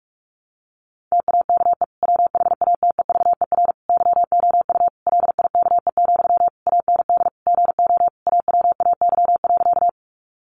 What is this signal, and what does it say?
Morse code, heard as a keyed tone, about 44 wpm, 700 Hz. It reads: TAKE WHATEVER YOU LIKE, AND GO AWAY.
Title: Sentences from Top 1100 Words 45wpm